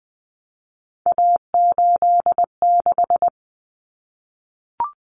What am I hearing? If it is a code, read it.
Morse code, heard as a keyed tone, about 20 wpm, 700 Hz. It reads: A86
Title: Common Cut Numbers - 3-Digit 20wpm